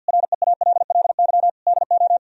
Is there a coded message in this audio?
Yes: Morse code from 0.1 to 2.2 s, heard as a keyed tone, about 64 wpm, 700 Hz. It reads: REALLY DO